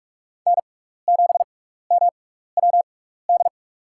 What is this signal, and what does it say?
Morse code, heard as a keyed tone, about 47 wpm, spaced as for 16 wpm, 690 Hz. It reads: N7MWD
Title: USA Call Signs - 5 Characters - 6x Character Spacing - 45wpm